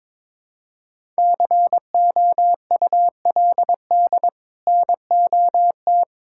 Morse code, heard as a keyed tone, about 22 wpm, 700 Hz. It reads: COULD NOT
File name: Build A Phrase - 2-Word Phrases - 22wpm